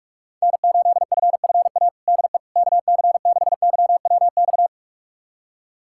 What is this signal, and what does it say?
Morse code, heard as a keyed tone, about 41 wpm, 700 Hz. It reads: N8FFA DE KC6YWX
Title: USA Callsign DE Callsign 45wpm